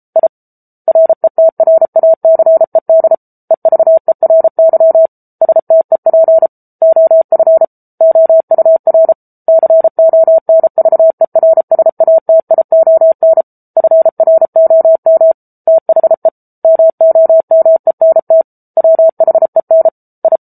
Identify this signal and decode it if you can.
Morse code, heard as a keyed tone, about 34 wpm, 650 Hz. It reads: I RETRACED EVERY STEP OF OUR CONVERSATION FROM THE MOMENT WHEN I
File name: Book Two / 07-Chapter 7 - 33 WPM